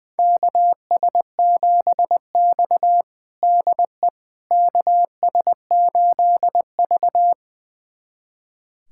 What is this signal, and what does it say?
Morse code, heard as a keyed tone, about 20 wpm, 700 Hz. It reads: KS7X DE KS8V
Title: USA Callsign DE Callsign - 4 Characters - 20wpm